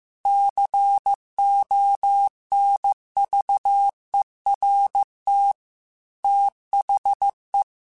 Morse code, heard as a keyed tone, about 15 wpm, 800 Hz. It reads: CONVERT THE
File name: Technical Subjects / Technical Subjects-mp3s / Sw Pwr Sup @13